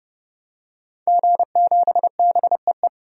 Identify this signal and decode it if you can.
Morse code, heard as a keyed tone, about 29 wpm, 700 Hz. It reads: G7BEE
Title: DXCC-100 Call Signs - 5 Characters - 30wpm